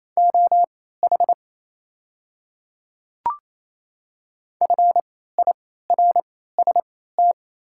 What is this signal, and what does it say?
Morse code, heard as a keyed tone, about 28 wpm, spaced as for 16 wpm, 690 Hz. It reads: OH FIRST
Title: 3x Character Spacing - Top 500 Words - 28wpm